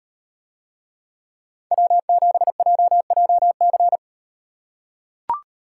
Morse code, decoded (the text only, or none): W7JJC